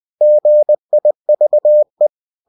Morse code, heard as a keyed tone, about 20 wpm, 600 Hz. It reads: GIVE
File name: cw-books / winnie-the-pooh / winnie-20wpm-page-0104